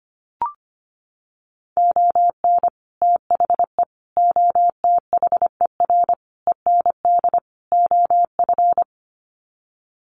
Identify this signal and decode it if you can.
Morse code, heard as a keyed tone, about 25 wpm, 700 Hz. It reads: ON THE OTHER END OF
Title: Sets of 5 Words Encore 25wpm